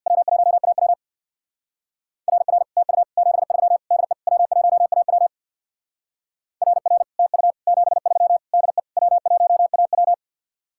Morse code, heard as a keyed tone, about 61 wpm, 700 Hz. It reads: W1AW RR TU 73 DE W1AW RR TU 73 DE W1AW
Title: Speed-Racing Common QSO Phrases - 50wpm